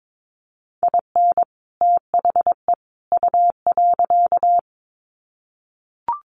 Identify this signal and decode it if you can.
Morse code, heard as a keyed tone, about 22 wpm, 700 Hz. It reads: IN THE U.
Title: Sets of 3 Words 22wpm